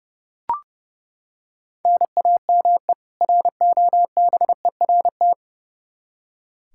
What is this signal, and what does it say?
Morse code, heard as a keyed tone, about 30 wpm, 700 Hz. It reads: NAME ROBERT
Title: Practice QSO - Exchange 1 - Beginning - 30wpm